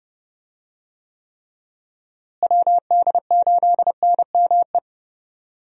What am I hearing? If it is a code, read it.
Morse code, heard as a keyed tone, about 30 wpm, 700 Hz. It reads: WD8NME
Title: USA Call Signs - 6 Characters - Repeated 2x - 30wpm